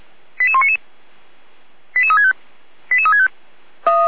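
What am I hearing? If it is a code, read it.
Five-tone selective call (ZVEI), heard as quick runs of five tones, tones of about 70 ms. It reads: 88199 88365 88365
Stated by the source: ZVEI five-tone selective call received with an SDR.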